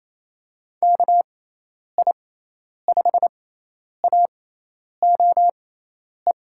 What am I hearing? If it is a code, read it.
Morse code, heard as a keyed tone, about 28 wpm, spaced as for 10 wpm, 700 Hz. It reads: KI5AOE
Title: USA Call Signs - 6 Characters - 6x Character Spacing - 28wpm